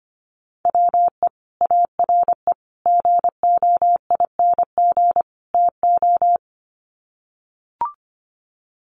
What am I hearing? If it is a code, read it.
Morse code, heard as a keyed tone, about 25 wpm, 700 Hz. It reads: WE ARE GOING TO
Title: Sets of 4 Words 25wpm